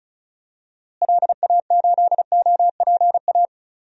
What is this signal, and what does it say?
Morse code, heard as a keyed tone, about 36 wpm, 700 Hz. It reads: LA8OPA
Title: DXCC-100 Call Signs - 6 Characters - Repeated 2x - 35wpm